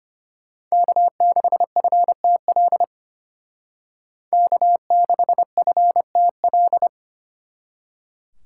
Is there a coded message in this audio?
Yes: Morse code from 0.7 to 6.9 s, heard as a keyed tone, about 33 wpm, 700 Hz. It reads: K6FTL K6FTL